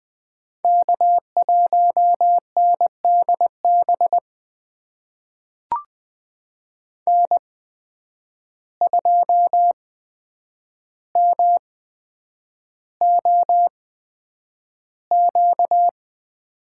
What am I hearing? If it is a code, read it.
Morse code, heard as a keyed tone, about 20 wpm, spaced as for 9 wpm, 700 Hz. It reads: K1NDB N2MOQ
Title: USA Call Signs - 5 Characters - 8x Character Spacing - 20wpm